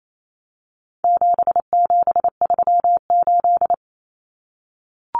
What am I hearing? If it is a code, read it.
Morse code, heard as a keyed tone, about 28 wpm, 700 Hz. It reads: 7738
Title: Numbers - 4-Digit 28wpm